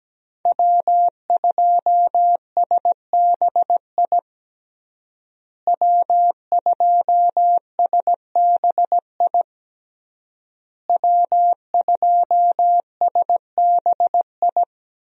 Morse code, heard as a keyed tone, about 17 wpm, 700 Hz. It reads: W2SBI W2SBI W2SBI
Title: USA Call Signs - 5 Characters - Repeated 3x - 17wpm